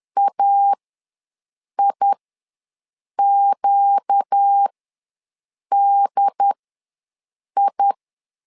Morse code, heard as a keyed tone, about 10 wpm, spaced as for 6 wpm, 800 Hz. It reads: AIQDI
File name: LEZIONI / 4 Q ws 10wpm 2306